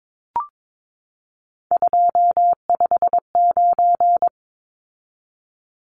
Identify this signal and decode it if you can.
Morse code, heard as a keyed tone, about 22 wpm, 700 Hz. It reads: 259
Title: Numbers - 3-Digit 22wpm